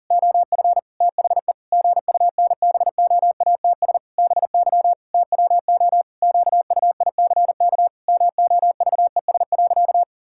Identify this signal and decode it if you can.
Morse code, heard as a keyed tone, about 40 wpm, 700 Hz. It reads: OF THE GUNBOATS BY TWO QUICK MOVES.
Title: Pangram Challenge v2 - 40wpm